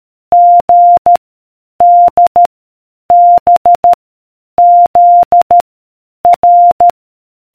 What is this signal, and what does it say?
Morse code, heard as a keyed tone, about 13 wpm, 700 Hz. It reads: G D B Z R